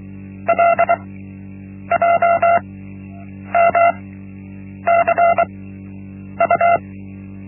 Morse code, heard as a keyed tone, about 23 wpm, spaced as for 8 wpm, 680 Hz. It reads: LJMCU
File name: Linksmakalnis-1